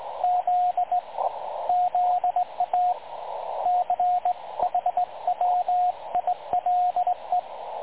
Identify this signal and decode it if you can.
Morse code, heard as a keyed tone, about 18 wpm, 720 Hz, with radio noise. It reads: Z ZA CHWILE
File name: dlazaawansowanych / nagraniazpasma / websdr recording 3565.0kHz